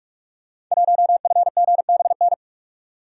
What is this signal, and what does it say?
Morse code, heard as a keyed tone, about 44 wpm, 700 Hz. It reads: 1FGBN
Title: Code Groups - 5 Letter:Numbers - 45wpm